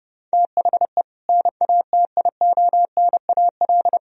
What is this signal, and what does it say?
Morse code, heard as a keyed tone, about 30 wpm, 700 Hz. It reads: THE NATIONAL
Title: Sentences from Top 1000 Words - Review Part 4 - 30wpm